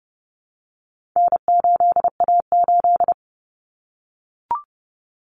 Morse code, heard as a keyed tone, about 30 wpm, 700 Hz. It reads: N8A8
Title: Common Cut Numbers - 4-Digit 30wpm